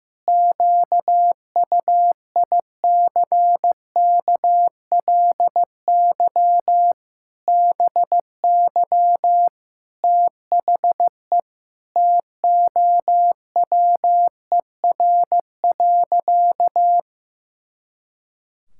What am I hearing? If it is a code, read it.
Morse code, heard as a keyed tone, about 15 wpm, 700 Hz. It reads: QUICKLY BY THE TOWER.
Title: Pangram Challenge v2 - 15wpm